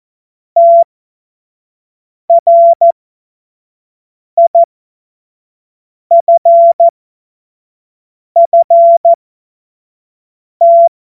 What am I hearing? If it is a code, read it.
Morse code, heard as a keyed tone, about 14 wpm, spaced as for 5 wpm, 680 Hz. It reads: TRIFFT
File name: Morsekurs / Witze / Witz 06. Juli 2021, 25 F70